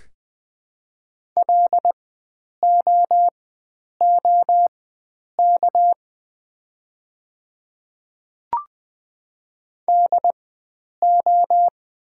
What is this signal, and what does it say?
Morse code, heard as a keyed tone, about 20 wpm, spaced as for 9 wpm, 700 Hz. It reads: LOOK DO
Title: 4x Character Spacing - Top 100 Words - 20wpm